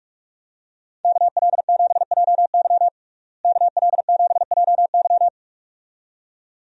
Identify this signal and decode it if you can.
Morse code, heard as a keyed tone, about 46 wpm, 700 Hz. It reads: KL7JY KL7JY
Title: DXCC-100 Call Signs - 5 Characters - Repeated 2x - 45wpm